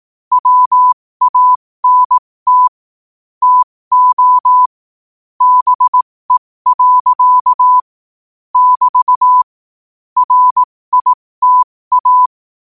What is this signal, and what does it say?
Morse code, heard as a keyed tone, about 18 wpm, spaced as for 15 wpm, 1000 Hz. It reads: WANT TO BE. <BT> RITA